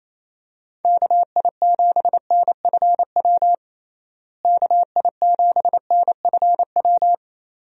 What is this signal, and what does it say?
Morse code, heard as a keyed tone, about 28 wpm, 700 Hz. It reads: KI7NFW KI7NFW
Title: USA Call Signs - 6 Characters - Repeated 2x - 28wpm